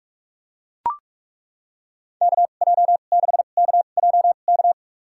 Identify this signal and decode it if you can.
Morse code, heard as a keyed tone, about 41 wpm, spaced as for 31 wpm, 700 Hz. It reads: KJ6KJK
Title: USA Call Signs - 2x Character Spacing - 45wpm